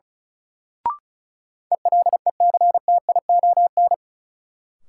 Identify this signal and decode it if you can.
Morse code, heard as a keyed tone, about 36 wpm, 700 Hz. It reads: ELECTION
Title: Sentences from Top 1500 Words - New Word Drill - 35wpm